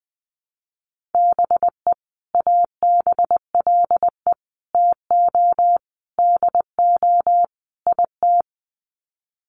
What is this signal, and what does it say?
Morse code, heard as a keyed tone, about 20 wpm, 700 Hz. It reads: BE ABLE TO DO IT
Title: Sets of 5 Words Encore2 20wpm